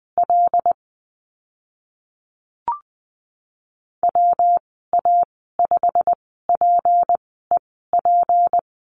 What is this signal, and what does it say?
Morse code, heard as a keyed tone, about 20 wpm, spaced as for 15 wpm, 700 Hz. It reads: L WA5PEP
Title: 173 Lesson 28g - Number 9 - Call Signs 20wpm